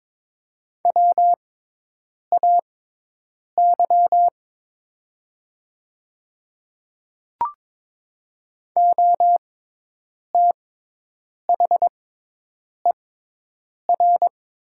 Morse code, heard as a keyed tone, about 22 wpm, spaced as for 8 wpm, 700 Hz. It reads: WAY OTHER
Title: 6x Character Spacing - Top 100 Words - 22wpm